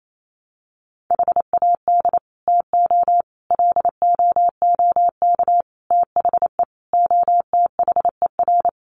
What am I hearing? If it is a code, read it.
Morse code, heard as a keyed tone, about 28 wpm, 700 Hz. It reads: HAD TO LOOK THE OTHER
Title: Sentences from Top 100 Words 28wpm